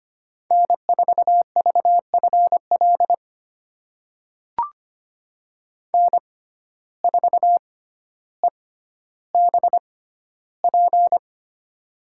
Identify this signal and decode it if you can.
Morse code, heard as a keyed tone, about 25 wpm, spaced as for 13 wpm, 700 Hz. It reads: N4VFL N4EBP
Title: USA Call Signs - 5 Characters - 6x Character Spacing - 25wpm